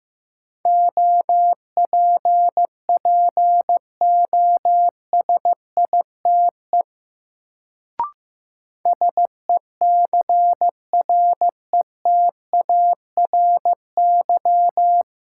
Morse code, heard as a keyed tone, about 15 wpm, 700 Hz. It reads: OPPOSITE SECRETARY
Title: Sentences from Top 1300 Words - New Word Drill Alt - 15wpm